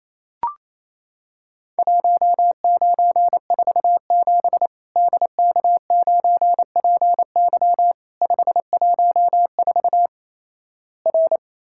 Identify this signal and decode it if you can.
Morse code, heard as a keyed tone, about 27 wpm, 700 Hz. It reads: 1947 DK9PY 514 R
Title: Worked All Europe DX Contest - QTC Exchange - 28wpm